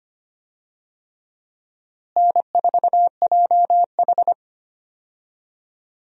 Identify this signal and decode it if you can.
Morse code, heard as a keyed tone, about 25 wpm, 700 Hz. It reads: N4JH
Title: USA Call Signs - 4 Characters - Repeated 3x - 25wpm